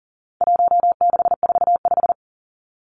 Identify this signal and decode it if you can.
Morse code, heard as a keyed tone, about 40 wpm, 700 Hz. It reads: 1645
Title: Numbers - 4-Digit 40wpm